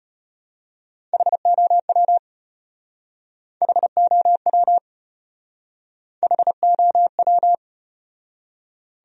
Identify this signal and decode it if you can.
Morse code, heard as a keyed tone, about 33 wpm, 700 Hz. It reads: HOW HOW HOW